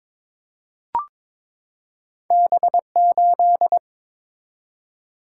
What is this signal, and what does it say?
Morse code, heard as a keyed tone, about 22 wpm, 700 Hz. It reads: B8
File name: practice / 22wpm / Two Letter-Number Combinations - 22wpm